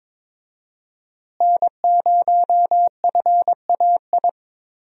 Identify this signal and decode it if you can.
Morse code, heard as a keyed tone, about 22 wpm, 700 Hz. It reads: N0FAI
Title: USA Call Signs - 5 Characters - Repeated 3x - 22wpm